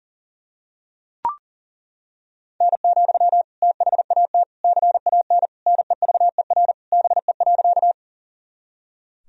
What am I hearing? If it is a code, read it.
Morse code, heard as a keyed tone, about 40 wpm, 700 Hz. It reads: N, THAT CAN NEVER BE.
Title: Sentences from Top 300 Words and Abbreviations 40wpm